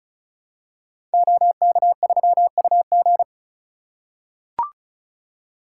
Morse code, heard as a keyed tone, about 36 wpm, 700 Hz. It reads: OK3UG
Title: Code Groups - 5 Letter:Numbers - 35wpm